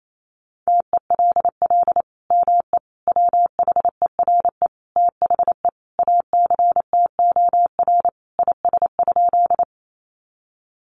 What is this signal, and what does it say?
Morse code, heard as a keyed tone, about 28 wpm, 700 Hz. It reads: TELL ME WHERE THE ACTOR IS?
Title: Sentences from Top 1500 Words 28wpm